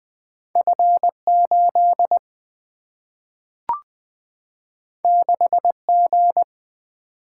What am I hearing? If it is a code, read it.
Morse code, heard as a keyed tone, about 20 wpm, 700 Hz. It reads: F8 6G
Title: Single-Letter Single-Number 20wpm